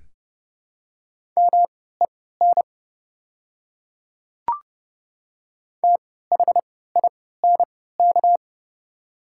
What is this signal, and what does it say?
Morse code, heard as a keyed tone, about 30 wpm, spaced as for 17 wpm, 700 Hz. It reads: MEN THINK
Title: 3x Character Spacing - Top 200 Words - 30wpm